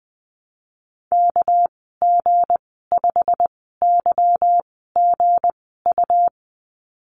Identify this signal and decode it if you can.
Morse code, heard as a keyed tone, about 20 wpm, spaced as for 15 wpm, 700 Hz. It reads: KG5YGU
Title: 213 Lesson 34g - Number 8 - Call Signs 20wpm